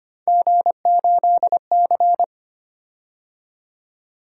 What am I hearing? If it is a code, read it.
Morse code, heard as a keyed tone, about 25 wpm, 700 Hz. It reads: G8C
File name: Three Letter-Number Combinations - 25wpm